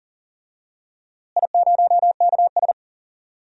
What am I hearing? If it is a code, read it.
Morse code, heard as a keyed tone, about 40 wpm, 700 Hz. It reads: I0KS
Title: DXCC-100 Call Signs - 4 Characters - Repeated 2x - 40wpm